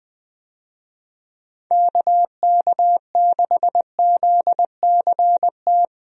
Morse code, heard as a keyed tone, about 20 wpm, 700 Hz. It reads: KK6ZCT
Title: USA Call Signs - 6 Characters - Repeated 3x - 20wpm